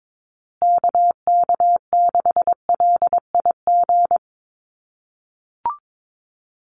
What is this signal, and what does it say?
Morse code, heard as a keyed tone, about 22 wpm, 700 Hz. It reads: KK6LIG
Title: USA Call Signs Repeated 3x 22wpm